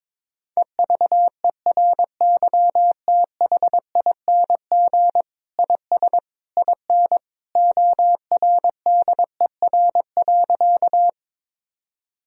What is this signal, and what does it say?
Morse code, heard as a keyed tone, about 22 wpm, 700 Hz. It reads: EVERYTHING IS IN ORDER.
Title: Sentences from Top 1000 Words - Review Part 2 - 22wpm